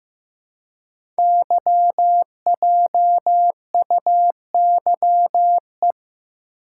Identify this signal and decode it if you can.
Morse code, heard as a keyed tone, about 15 wpm, 700 Hz. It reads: YJUYE
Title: Code Groups - 5 Letters - 15wpm